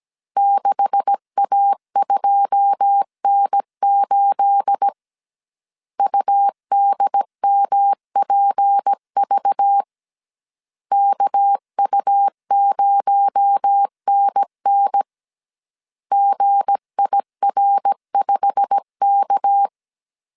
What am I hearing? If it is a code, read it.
Morse code, heard as a keyed tone, about 17 wpm, 800 Hz. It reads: 6A2N8 UDMPV KU0NN GIR5K
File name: CW / DET / 1 9 1500C S A 16wpm 2828